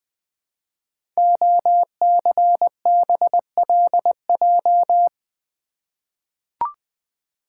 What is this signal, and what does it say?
Morse code, heard as a keyed tone, about 20 wpm, 700 Hz. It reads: OCBLJ
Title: Code Groups - 5 Letters - 20wpm